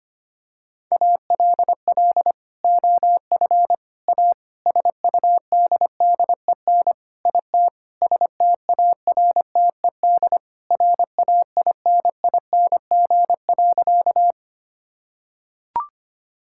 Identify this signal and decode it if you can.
Morse code, heard as a keyed tone, about 25 wpm, 700 Hz. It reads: ALL OF A SUDDEN IT STARTED RAINING.
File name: practice / 25wpm / Sentences with 4-Word Phrases - 25wpm